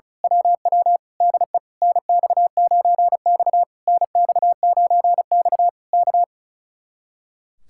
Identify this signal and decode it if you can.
Morse code, heard as a keyed tone, about 33 wpm, 700 Hz. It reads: WW DE NX9X NX9X K
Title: Speed-Racing Common QSO Phrases Encore - 35wpm